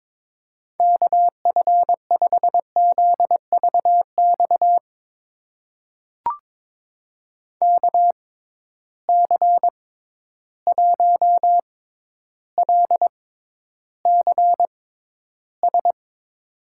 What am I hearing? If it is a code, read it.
Morse code, heard as a keyed tone, about 22 wpm, spaced as for 11 wpm, 700 Hz. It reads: KF5ZVX KC1LCS
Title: USA Call Signs - 6 Characters - 6x Character Spacing - 22wpm